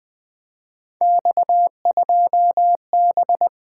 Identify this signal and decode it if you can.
Morse code, heard as a keyed tone, about 20 wpm, 700 Hz. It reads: X2B
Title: Three Letter-Number Combinations - 20wpm